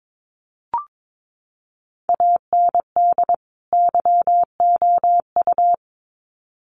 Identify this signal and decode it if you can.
Morse code, heard as a keyed tone, about 22 wpm, 700 Hz. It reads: AND YOU
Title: Sets of 2 Words 22wpm